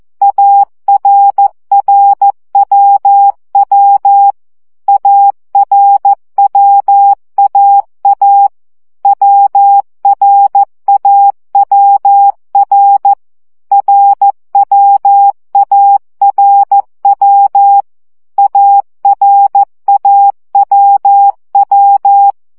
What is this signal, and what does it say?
Morse code, heard as a keyed tone, about 14 wpm, 800 Hz. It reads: ARRWW ARWAA WRAWR RWARW ARAWW
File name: CW gyakorlas a r w 12 baud